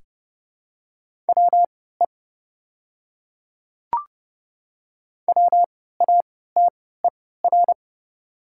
Morse code, heard as a keyed tone, about 31 wpm, spaced as for 17 wpm, 700 Hz. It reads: WE WATER
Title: 3x Character Spacing - Top 100 Words - 30wpm